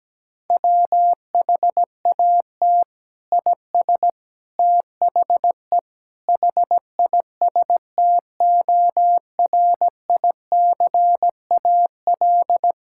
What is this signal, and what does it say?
Morse code, heard as a keyed tone, about 17 wpm, 700 Hz. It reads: WHAT IS THE HISTORICAL
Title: Sentences from Top 1000 Words - Review Part 5 - 17wpm